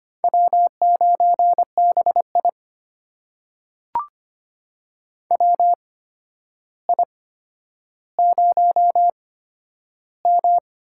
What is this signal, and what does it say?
Morse code, heard as a keyed tone, about 25 wpm, spaced as for 11 wpm, 700 Hz. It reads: W9BI WI0M
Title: USA Call Signs - 4 Characters - 8x Character Spacing - 25wpm